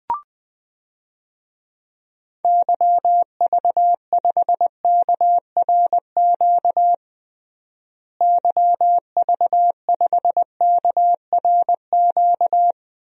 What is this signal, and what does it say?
Morse code, heard as a keyed tone, about 21 wpm, 700 Hz. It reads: YV5KRQ YV5KRQ
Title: DXCC-100 Call Signs - 6 Characters - Repeated 2x - 20wpm